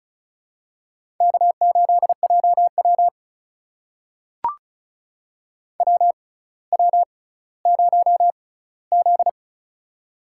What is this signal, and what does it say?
Morse code, heard as a keyed tone, about 36 wpm, spaced as for 18 wpm, 700 Hz. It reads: K8JW WW0Z
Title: USA Call Signs - 4 Characters - 6x Character Spacing - 35wpm